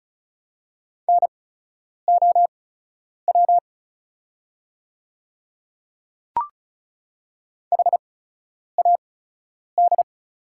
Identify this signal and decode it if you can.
Morse code, heard as a keyed tone, about 36 wpm, spaced as for 10 wpm, 690 Hz. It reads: NOW HAD
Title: 8x Character Spacing - Top 100 Words - 35wpm